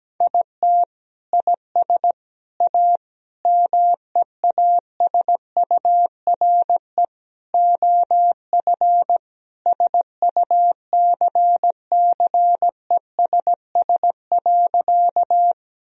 Morse code, heard as a keyed tone, about 17 wpm, 700 Hz. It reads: IT IS A MEASURE OF SUCCESS.
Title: Sentences from Top 1000 Words - Review Part 2 - 17wpm